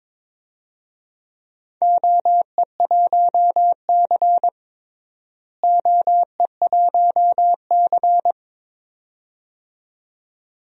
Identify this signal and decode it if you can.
Morse code, heard as a keyed tone, about 22 wpm, 700 Hz. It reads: OE1C OE1C
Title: DXCC-100 Call Signs - 4 Characters - Repeated 2x - 22wpm